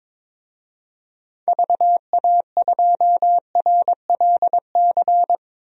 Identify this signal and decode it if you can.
Morse code, heard as a keyed tone, about 22 wpm, 700 Hz. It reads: VA2RLC
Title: DXCC-100 Call Signs - 6 Characters - 22wpm